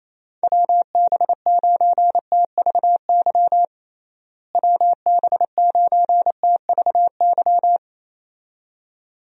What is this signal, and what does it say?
Morse code, heard as a keyed tone, about 28 wpm, 700 Hz. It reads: WB9TVY WB9TVY
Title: USA Call Signs - 6 Characters - Repeated 2x - 28wpm